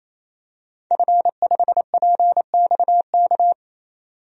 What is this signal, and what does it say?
Morse code, heard as a keyed tone, about 28 wpm, 700 Hz. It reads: F5PXK
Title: DXCC-100 Call Signs - 5 Characters - 28wpm